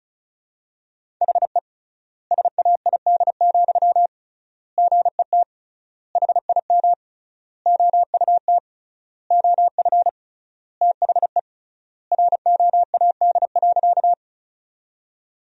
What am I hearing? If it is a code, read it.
Morse code, heard as a keyed tone, about 36 wpm, 700 Hz. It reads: HE SAID, GET HIM OUT OF THE ROAD.
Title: 3x Word-Spacing - Sentences from Top 400 Words - 35wpm